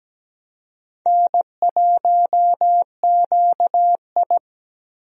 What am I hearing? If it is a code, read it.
Morse code, heard as a keyed tone, about 17 wpm, 700 Hz. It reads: N1QI
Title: USA Call Signs - 4 Characters - Repeated 3x - 17wpm